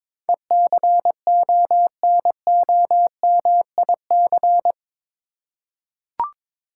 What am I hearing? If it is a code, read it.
Morse code, heard as a keyed tone, about 21 wpm, 700 Hz. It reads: ECONOMIC
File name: practice / 22wpm / Eight Letter Words - 22wpm